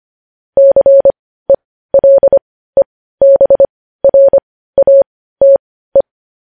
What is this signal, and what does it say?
Morse code, heard as a keyed tone, about 25 wpm, spaced as for 15 wpm, 550 Hz. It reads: CELEBRATE
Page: News Headlines in Morse Code at 15 WPM